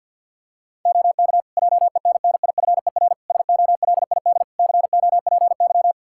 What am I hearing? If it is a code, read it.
Morse code, heard as a keyed tone, about 46 wpm, 700 Hz. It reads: OK JENNIFER SOLID COPY
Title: Practice QSO - Exchange 3 - More Info - 50wpm